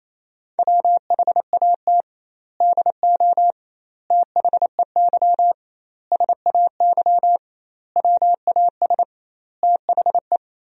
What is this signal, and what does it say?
Morse code, heard as a keyed tone, about 28 wpm, 700 Hz. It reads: WHAT DO THEY SAY WAS THE
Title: 2x Word-Spacing - Sentences from Top 200 Words - 28wpm